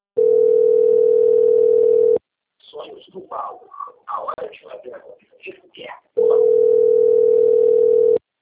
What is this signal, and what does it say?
Ringback tone (North America), 440 + 480 Hz, with speech.
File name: out- - - - - .73549